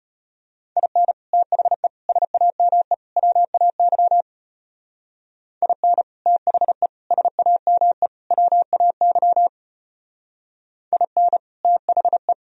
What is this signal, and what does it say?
Morse code, heard as a keyed tone, about 30 wpm, 700 Hz. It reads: IN THE SAME WAY IN THE SAME WAY IN THE
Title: Speed-Racing - Sets of 4 Words 25wpm